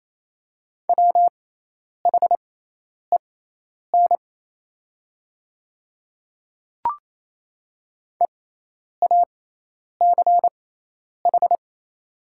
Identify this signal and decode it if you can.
Morse code, heard as a keyed tone, about 27 wpm, spaced as for 10 wpm, 690 Hz. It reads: WHEN EACH
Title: 6x Character Spacing - Top 100 Words - 28wpm